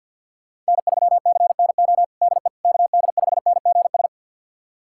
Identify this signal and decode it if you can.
Morse code, heard as a keyed tone, about 46 wpm, 700 Hz. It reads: N3CNO DE KD5NGS
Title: USA Callsign DE Callsign 50wpm